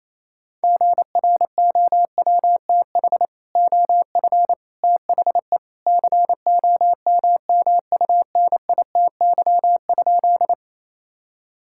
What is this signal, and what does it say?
Morse code, heard as a keyed tone, about 28 wpm, 700 Hz. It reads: GROWTH OF THE COMMUNITY?
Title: Sentences from Top 1000 Words - Review Part 5 - 28wpm